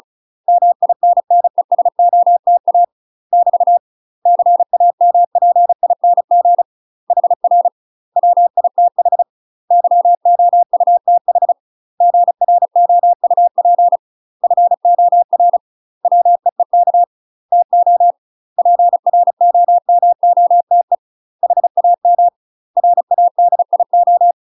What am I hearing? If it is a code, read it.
Morse code, heard as a keyed tone, about 36 wpm, 700 Hz. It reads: MINNESOTA <BT> CAMPING HR WITH YOUTH GROUP FOR WEEK TO PROMOTE HAM RADIO
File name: UK MASTER QSOs TRAINING / 35WPM / UK MASTER QSOs 0011